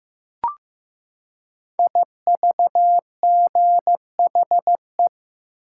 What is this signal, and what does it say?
Morse code, heard as a keyed tone, about 15 wpm, 700 Hz. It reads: IVGHE
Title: Code Groups - 5 Letters - 15wpm